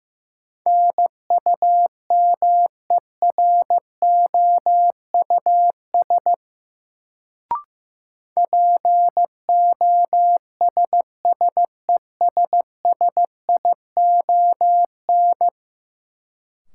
Morse code, heard as a keyed tone, about 15 wpm, 700 Hz. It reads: NUMEROUS POSSESSION
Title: Sentences from Top 1400 Words - New Word Drill Alt - 15wpm